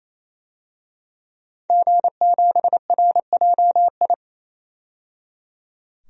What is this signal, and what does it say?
Morse code, heard as a keyed tone, about 28 wpm, 700 Hz. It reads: G7RJI
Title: DXCC-100 Call Signs - 5 Characters - 28wpm